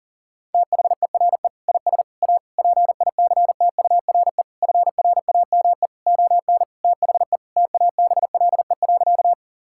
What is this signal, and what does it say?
Morse code, heard as a keyed tone, about 40 wpm, 700 Hz. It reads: THERE IS A PICTURE FRAME ON THE TABLE.
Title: Sentences from Top 1000 Words - Review Part 6 - 40wpm